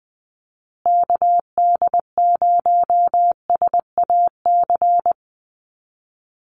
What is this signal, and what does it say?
Morse code, heard as a keyed tone, about 19 wpm, 700 Hz. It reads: KD0SAC
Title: USA Call Signs Repeated 2x 20wpm